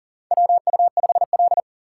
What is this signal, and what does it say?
Morse code, heard as a keyed tone, about 43 wpm, 690 Hz. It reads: WU5L
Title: USA Call Signs - 4 Characters - 6x Character Spacing - 40wpm